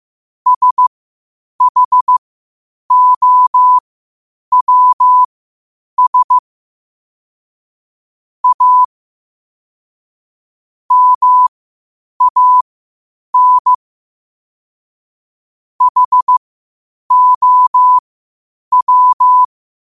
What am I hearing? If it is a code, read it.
Morse code, heard as a keyed tone, about 15 wpm, spaced as for 8 wpm, 1000 Hz. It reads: SHOWS A MAN HOW
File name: qotdfiles / 8wpm